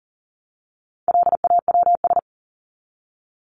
Morse code, heard as a keyed tone, about 40 wpm, 700 Hz. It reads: LAWS
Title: ES:S Suffix - Top 100 Words with Base Words 40wpm